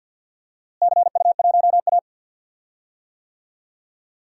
Morse code, heard as a keyed tone, about 47 wpm, 700 Hz. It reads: CU1A
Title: DXCC-100 Call Signs - 4 Characters - Repeated 3x - 50wpm